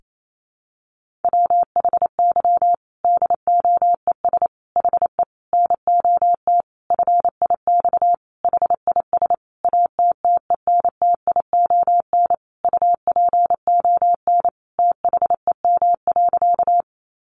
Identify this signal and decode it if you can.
Morse code, heard as a keyed tone, about 28 wpm, 700 Hz. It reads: WHY DOES HE NOT FIX HIS ATTENTION UPON THEM.
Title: Sentences from Top 1400 Words 28wpm